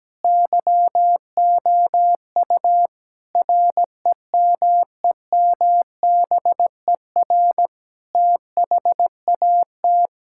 Morse code, heard as a keyed tone, about 17 wpm, 700 Hz. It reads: YOU REMEMBER THAT
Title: Sentences from Top 1000 Words - Review Part 5 - 17wpm